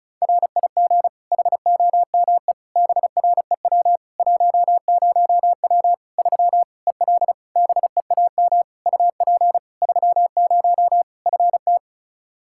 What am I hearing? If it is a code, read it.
Morse code, heard as a keyed tone, about 36 wpm, 700 Hz. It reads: RIG HOME BREW 10W 3 EL BEAM UP 30 FT
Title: Practice QSO - Exchange 2 - Info - 35wpm